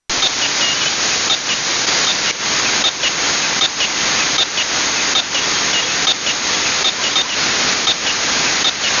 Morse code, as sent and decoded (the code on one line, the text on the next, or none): none